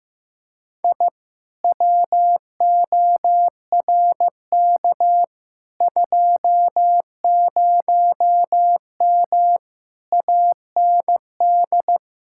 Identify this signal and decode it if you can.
Morse code, heard as a keyed tone, about 15 wpm, 700 Hz. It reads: I WORK 20M AND
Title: Speed-Racing Common QSO Phrases Encore - 15wpm